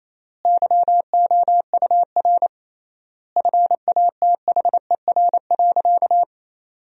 Morse code, heard as a keyed tone, about 28 wpm, 700 Hz. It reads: YOUR FATHER.
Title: 3x Word-Spacing - Sentences from Top 300 Words - 28wpm